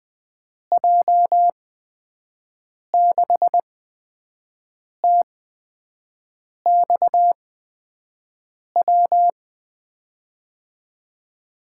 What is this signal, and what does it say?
Morse code, heard as a keyed tone, about 20 wpm, spaced as for 5 wpm, 700 Hz. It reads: J6TXW
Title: USA Call Signs - 8x Character Spacing - 20wpm